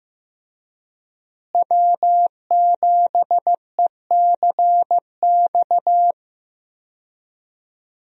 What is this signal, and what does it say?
Morse code, heard as a keyed tone, about 15 wpm, 700 Hz. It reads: W7ECX
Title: USA Call Signs - 5 Characters - Repeated 2x - 15wpm